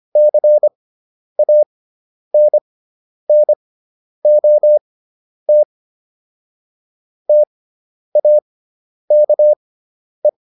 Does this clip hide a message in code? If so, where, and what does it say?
Yes: Morse code from 0.2 to 10.3 s, heard as a keyed tone, about 25 wpm, spaced as for 10 wpm, 600 Hz. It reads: CANNOT TAKE